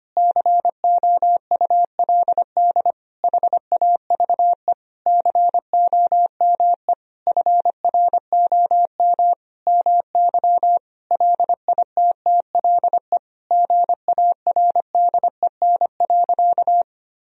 Morse code, heard as a keyed tone, about 25 wpm, 700 Hz. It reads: COULD HAVE COME FROM MY LITTLE GARDEN.
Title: Sentences from Top 1000 Words - Review Part 4 - 25wpm